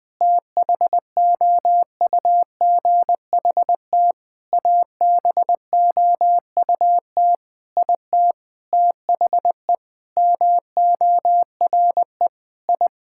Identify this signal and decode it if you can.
Morse code, heard as a keyed tone, about 20 wpm, 700 Hz. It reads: THOUGHT ABOUT IT THE MORE I
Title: Sentences with 9-Word Phrases - Spoken Beginning - 20wpm